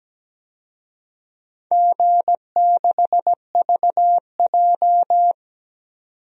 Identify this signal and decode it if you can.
Morse code, heard as a keyed tone, about 17 wpm, 700 Hz. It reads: G6VJ
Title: DXCC-100 Call Signs - 4 Characters - 17wpm